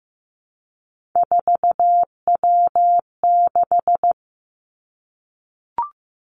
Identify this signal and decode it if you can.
Morse code, heard as a keyed tone, about 15 wpm, 700 Hz. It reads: 4W6
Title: Three Challenging Letter-Number Combinations 15wpm